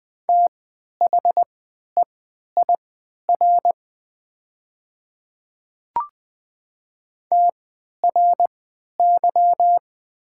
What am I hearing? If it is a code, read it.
Morse code, heard as a keyed tone, about 20 wpm, spaced as for 11 wpm, 700 Hz. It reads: THEIR TRY
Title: 3x Character Spacing - Top 200 Words - 20wpm